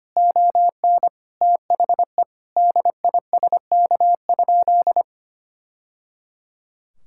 Morse code, heard as a keyed tone, about 25 wpm, 700 Hz. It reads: ON THE DISK?
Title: Sentences from Top 1000 Words - Review Part 6 - 25wpm